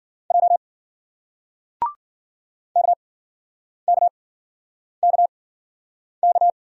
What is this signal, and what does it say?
Morse code, heard as a keyed tone, about 56 wpm, spaced as for 9 wpm, 700 Hz. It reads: J KKKK